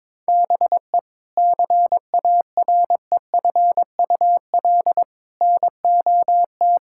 Morse code, heard as a keyed tone, about 22 wpm, 700 Hz. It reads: BE CAREFUL NOT
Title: Sentences from Top 1600 Words 22wpm